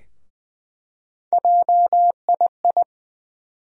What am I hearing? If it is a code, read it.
Morse code, heard as a keyed tone, about 20 wpm, 700 Hz. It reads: JII
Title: USA Call Sign Postfixes - 3 Letters 20wpm